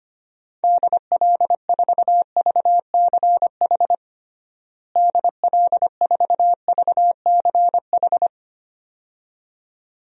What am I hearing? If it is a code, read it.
Morse code, heard as a keyed tone, about 25 wpm, 700 Hz. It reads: DL4VCH DL4VCH
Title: DXCC-100 Call Signs - 6 Characters - Repeated 2x - 25wpm